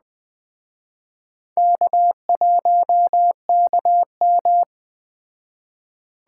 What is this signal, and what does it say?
Morse code, heard as a keyed tone, about 20 wpm, 700 Hz. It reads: K1KM